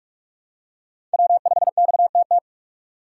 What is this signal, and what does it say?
Morse code, heard as a keyed tone, about 47 wpm, 700 Hz. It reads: W5XTT
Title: USA Call Signs - 3x Character Spacing - 45wpm